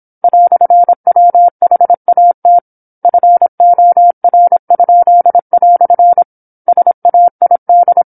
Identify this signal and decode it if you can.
Morse code, heard as a keyed tone, about 27 wpm, 700 Hz. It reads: "WHAT FOR?" SAID
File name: 08-Chapter-08 26 WPM